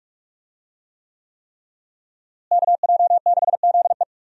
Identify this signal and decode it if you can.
Morse code, heard as a keyed tone, about 42 wpm, 700 Hz. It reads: KJ6ZE
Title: USA Call Signs - 5 Characters - Repeated 3x - 45wpm